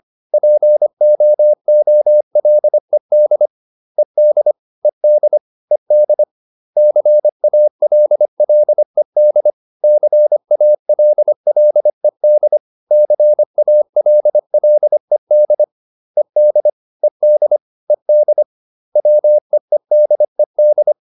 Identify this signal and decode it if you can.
Morse code, heard as a keyed tone, about 25 wpm, 600 Hz. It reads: POOLED ED ED ED CALLED CALLED CALLED ED ED ED WEEDED